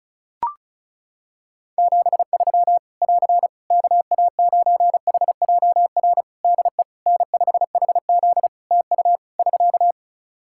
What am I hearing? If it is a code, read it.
Morse code, heard as a keyed tone, about 35 wpm, 700 Hz. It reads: 73 <AR> KA9HJR DE N5HZ TU <SK>
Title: Practice QSO - Exchange 4 - Ending - 35wpm